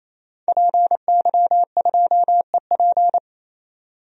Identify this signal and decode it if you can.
Morse code, heard as a keyed tone, about 28 wpm, 700 Hz. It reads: PY2EP
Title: DXCC-100 Call Signs - 5 Characters - 28wpm